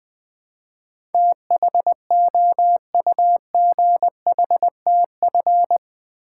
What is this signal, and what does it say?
Morse code, heard as a keyed tone, about 21 wpm, 700 Hz. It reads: THOUGHTF
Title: Build A Word - 4 Letters At A Time - 10-Letter Words - 20wpm